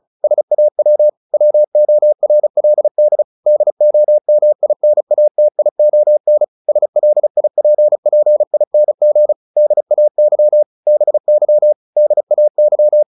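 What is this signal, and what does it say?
Morse code, heard as a keyed tone, about 34 wpm, 600 Hz. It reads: SAW WORLD DOMINATION SLIPPING DAY BY DAY
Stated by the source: Morse practice at 35 wpm from CW med Gnister, sending Ken Follett's novel Never in English.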